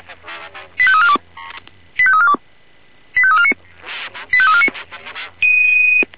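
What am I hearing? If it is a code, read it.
Five-tone selective call (ZVEI), heard as quick runs of five tones, tones of about 70 ms. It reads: 85342 85342 85349 85349
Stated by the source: ZVEI five-tone call 85886 received with an SDR.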